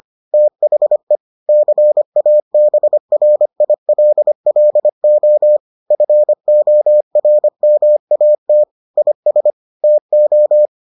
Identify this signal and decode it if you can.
Morse code, heard as a keyed tone, about 25 wpm, 600 Hz. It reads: THE CABRILLO FORMAT IS TO